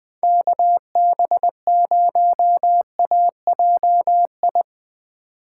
Morse code, heard as a keyed tone, about 21 wpm, 700 Hz. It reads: KB0AJI
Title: USA Call Signs - 6 Characters - 2x Character Spacing - 20wpm